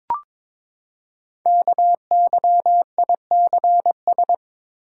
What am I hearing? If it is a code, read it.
Morse code, heard as a keyed tone, about 22 wpm, 700 Hz. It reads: KYICS
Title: Code Groups - 5 Letters - 22wpm